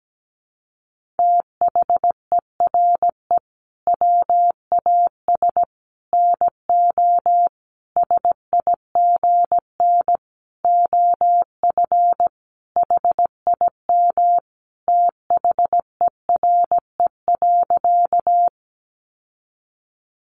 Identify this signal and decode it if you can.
Morse code, heard as a keyed tone, about 17 wpm, 700 Hz. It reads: THERE WAS NO SIGN OF HIM THERE.
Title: Sentences from Top 1000 Words 17wpm